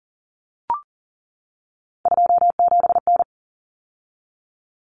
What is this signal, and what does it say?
Morse code, heard as a keyed tone, about 40 wpm, 700 Hz. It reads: 27N
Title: Common Cut Numbers - 3-Digit 40wpm